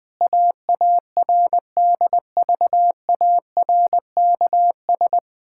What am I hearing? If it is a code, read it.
Morse code, heard as a keyed tone, about 20 wpm, 700 Hz. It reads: AARDVARKS
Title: Pangram Challenge v2 - 20wpm